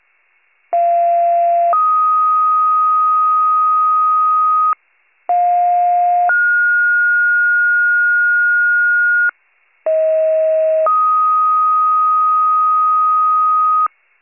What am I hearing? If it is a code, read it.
Two-tone page, heard as a one-second tone, then a three-second tone, about 690 then 1250 Hz, about 710 then 1430 Hz, about 640 then 1250 Hz. It is heard from 0.7 to 13.9 s.